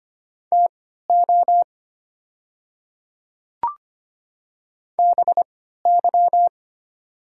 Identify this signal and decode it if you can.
Morse code, heard as a keyed tone, about 25 wpm, spaced as for 14 wpm, 700 Hz. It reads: TO BY
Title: 3x Character Spacing - Top 100 Words - 25wpm